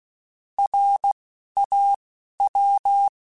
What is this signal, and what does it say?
Morse code, heard as a keyed tone, about 16 wpm, 800 Hz. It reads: R A W